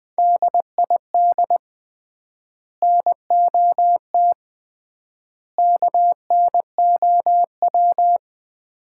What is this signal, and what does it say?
Morse code, heard as a keyed tone, about 20 wpm, 700 Hz. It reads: DID NOT KNOW
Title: 3x Word-Spacing - Sets of 3 Words - 20wpm